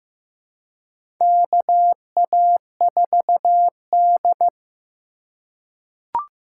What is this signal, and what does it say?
Morse code, heard as a keyed tone, about 15 wpm, 700 Hz. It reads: KA4D
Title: USA Call Signs - 4 Characters - 2x Character Spacing - 15wpm